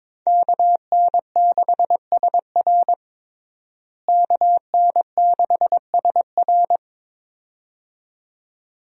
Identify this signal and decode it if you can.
Morse code, heard as a keyed tone, about 21 wpm, 700 Hz. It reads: KN6SR KN6SR
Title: USA Call Signs - 5 Characters - Repeated 2x - 22wpm